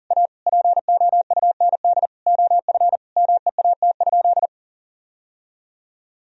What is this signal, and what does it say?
Morse code, heard as a keyed tone, about 40 wpm, 700 Hz. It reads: A POUND OF MEAT?
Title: Sentences from Top 1000 Words - Review Part 3 - 40wpm